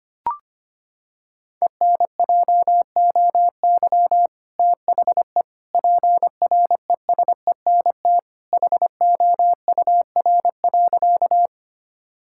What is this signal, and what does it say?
Morse code, heard as a keyed tone, about 25 wpm, 700 Hz. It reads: ENJOY THE PRESENT HOUR.
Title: Sentences from Top 1000 Words - Review Part 5 - 25wpm